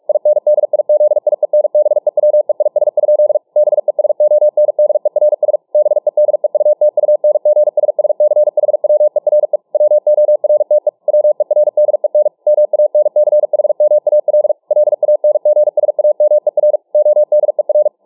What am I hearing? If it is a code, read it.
Morse code, heard as a keyed tone, about 46 wpm, 600 Hz. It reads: INDIZIENBEWEIS? BESONDERS BEDEUTUNGSSCHWERE WORTE WERDEN MANCHMAL LANGSAMER ODER